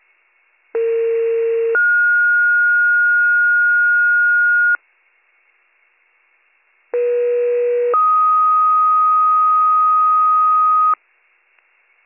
Two-tone page, heard as a one-second tone, then a three-second tone, about 460 then 1430 Hz, about 480 then 1250 Hz. It is heard from 0.7 to 10.9 s.